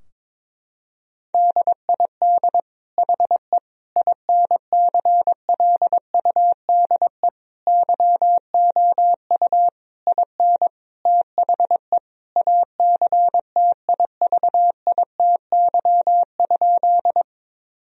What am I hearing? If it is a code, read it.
Morse code, heard as a keyed tone, about 22 wpm, 700 Hz. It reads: DID HE INCLUDE YOU IN THE ACTIVITY?
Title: Sentences from Top 1000 Words - Review Part 4 - 22wpm